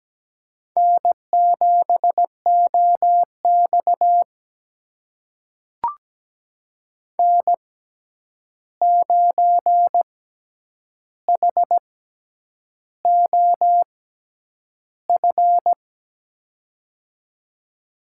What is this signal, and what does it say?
Morse code, heard as a keyed tone, about 17 wpm, spaced as for 8 wpm, 700 Hz. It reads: N7OX N9HOF
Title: USA Call Signs - 6x Character Spacing - 17wpm